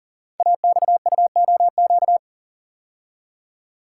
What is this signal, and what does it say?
Morse code, heard as a keyed tone, about 38 wpm, 690 Hz. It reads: AXUOQ
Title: Code Groups - 5 Letters - 40wpm